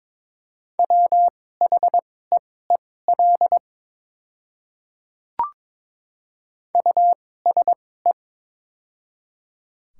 Morse code, heard as a keyed tone, about 21 wpm, spaced as for 16 wpm, 700 Hz. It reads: WHEEL USE
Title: 2x Character Spacing - Top 500 Words - 22wpm